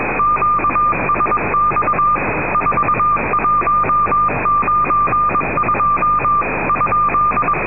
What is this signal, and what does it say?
Morse code, heard as a keyed tone, about 21 wpm, 1200 Hz, with noise. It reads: QSX 4192 ?
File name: europe / rus / rus-4290-sankt petersburg radio urd-whe-1992